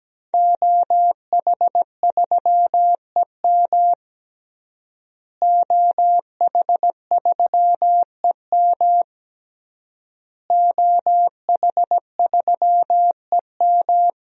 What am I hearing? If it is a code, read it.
Morse code, heard as a keyed tone, about 17 wpm, 700 Hz. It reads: OH3EM OH3EM OH3EM
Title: DXCC-100 Call Signs - 5 Characters - Repeated 3x - 17wpm